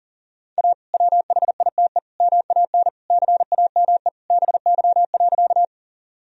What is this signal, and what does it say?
Morse code, heard as a keyed tone, about 37 wpm, 700 Hz. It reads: A WHITE MAN CAME BY.
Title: Sentences from Top 1000 Words - Review Part 1 - 40wpm